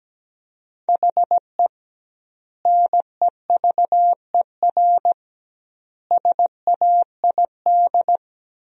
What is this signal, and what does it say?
Morse code, heard as a keyed tone, about 17 wpm, 700 Hz. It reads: HE NEVER SAID
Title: 2x Word-Spacing - Sentences from Top 500 Words - 17wpm